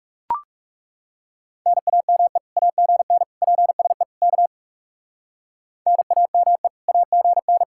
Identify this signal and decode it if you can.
Morse code, heard as a keyed tone, about 33 wpm, 700 Hz. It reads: NAME AGN PSE K NAME AGN
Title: Speed-Racing Common QSO Phrases - 30wpm